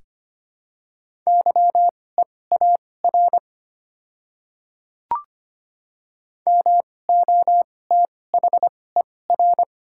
Morse code, heard as a keyed tone, about 25 wpm, spaced as for 18 wpm, 690 Hz. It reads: YEAR MOTHER
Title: 2x Character Spacing - Top 200 Words - 25wpm